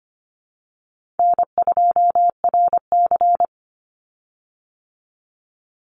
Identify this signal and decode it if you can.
Morse code, heard as a keyed tone, about 25 wpm, 700 Hz. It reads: N2RC
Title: Easy Call Signs - 25wpm